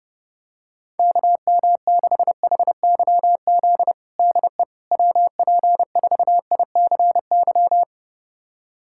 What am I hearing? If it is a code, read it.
Morse code, heard as a keyed tone, about 31 wpm, 700 Hz. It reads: KM6HYZ DE WP4ICY
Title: USA Callsign DE Callsign - 6 Characters - 30wpm